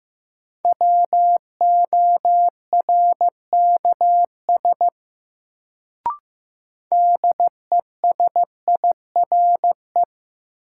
Morse code, heard as a keyed tone, about 15 wpm, 700 Hz. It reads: WORKS DESIRE
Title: Sentences from Top 1100 Words - New Word Drill Alt - 15wpm